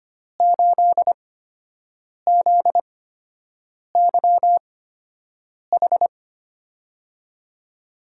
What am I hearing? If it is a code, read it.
Morse code, heard as a keyed tone, about 25 wpm, spaced as for 7 wpm, 700 Hz. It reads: 8ZYH
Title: USA Call Signs - 8x Character Spacing - 25wpm